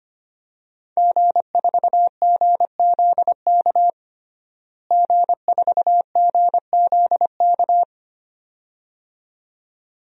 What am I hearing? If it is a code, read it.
Morse code, heard as a keyed tone, about 25 wpm, 700 Hz. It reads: G4GZK G4GZK
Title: DXCC-100 Call Signs - 5 Characters - Repeated 2x - 25wpm